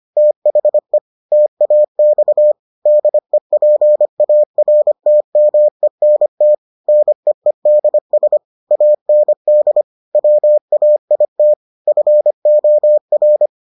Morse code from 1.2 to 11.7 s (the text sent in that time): TAX DEPARTMENT NEEDS AND WAIT